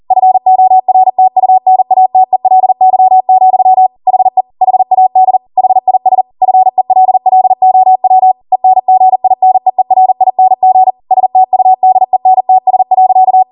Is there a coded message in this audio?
Yes: Morse code from 0.1 to 13.4 s, heard as a keyed tone, about 40 wpm, 750 Hz. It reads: FORTUNATELY, HE HAD HIS FELLOW ENGINEERING STUDENTS.